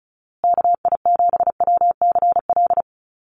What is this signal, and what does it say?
Morse code, heard as a keyed tone, about 34 wpm, 700 Hz. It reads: KI7WCL
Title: USA Call Signs Repeated 3x 35wpm